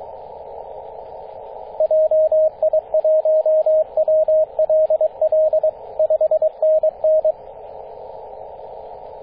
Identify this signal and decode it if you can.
Morse code, heard as a keyed tone, about 22 wpm, 630 Hz, with radio noise. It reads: JI1WLL 5NN
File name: 9m0o-24-891mhz-cw